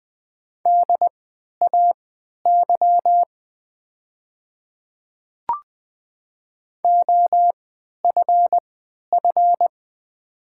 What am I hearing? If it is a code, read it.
Morse code, heard as a keyed tone, about 20 wpm, spaced as for 11 wpm, 690 Hz. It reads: DAY OFF
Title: 3x Character Spacing - Top 200 Words - 20wpm